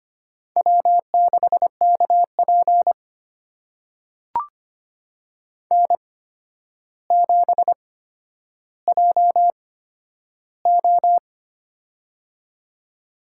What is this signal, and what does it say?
Morse code, heard as a keyed tone, about 25 wpm, spaced as for 11 wpm, 700 Hz. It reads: W6KP N7JO
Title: USA Call Signs - 4 Characters - 8x Character Spacing - 25wpm